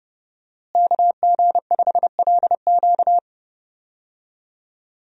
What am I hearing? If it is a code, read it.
Morse code, heard as a keyed tone, about 32 wpm, 700 Hz. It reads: KG5LQ
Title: USA Call Signs - 5 Characters - Repeated 3x - 30wpm